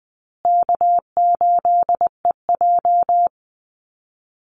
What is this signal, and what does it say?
Morse code, heard as a keyed tone, about 20 wpm, 700 Hz. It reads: K8EJ
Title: Call Signs Repeated 2x - 20wpm